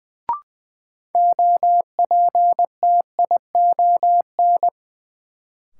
Morse code, heard as a keyed tone, about 20 wpm, 700 Hz. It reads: OPTION
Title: Sentences from Top 1300 Words - New Word Drill - 20wpm